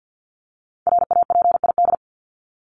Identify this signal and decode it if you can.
Morse code, heard as a keyed tone, about 48 wpm, 700 Hz. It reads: RAPID
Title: LY Suffix - Top 100 Words with Base Words 50wpm